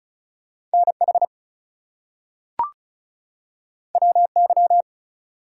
Morse code, heard as a keyed tone, about 36 wpm, 700 Hz. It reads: NH WY
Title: US States and Canadian Provinces 35wpm